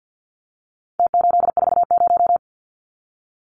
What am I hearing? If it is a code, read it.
Morse code, heard as a keyed tone, about 50 wpm, 700 Hz. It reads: T840
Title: Numbers - Contest Style - 4-Digit 50wpm